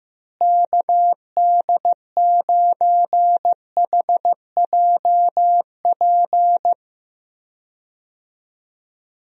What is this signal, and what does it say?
Morse code, heard as a keyed tone, about 15 wpm, 700 Hz. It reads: KD9HJP
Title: USA Call Signs - 6 Characters - Repeated 2x - 15wpm